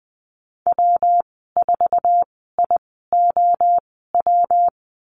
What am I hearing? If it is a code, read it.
Morse code, heard as a keyed tone, about 20 wpm, spaced as for 14 wpm, 700 Hz. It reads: W4IOW
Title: 179 Lesson 29g - Slash - Call Signs 20wpm